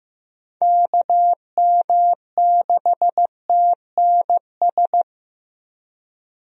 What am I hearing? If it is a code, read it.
Morse code, heard as a keyed tone, about 15 wpm, 700 Hz. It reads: KM6TNS
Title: USA Call Signs - 6 Characters - Repeated 2x - 15wpm